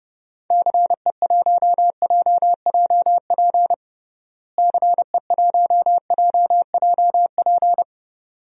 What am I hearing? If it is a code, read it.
Morse code, heard as a keyed tone, about 32 wpm, 700 Hz. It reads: CE1JJP CE1JJP
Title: DXCC-100 Call Signs - 6 Characters - Repeated 2x - 30wpm